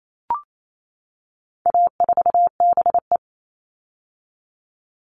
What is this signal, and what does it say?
Morse code, heard as a keyed tone, about 28 wpm, 700 Hz. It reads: A4BE
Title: All Cut Numbers - 4-Digit 28wpm